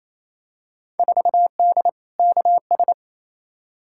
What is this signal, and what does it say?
Morse code, heard as a keyed tone, about 28 wpm, 700 Hz. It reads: 4D KS
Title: Field Day Exchanges Encore - 28wpm